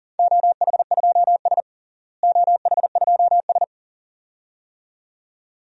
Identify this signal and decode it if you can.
Morse code, heard as a keyed tone, about 40 wpm, 700 Hz. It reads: OH2S OH2S
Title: DXCC-100 Call Signs - 4 Characters - Repeated 2x - 40wpm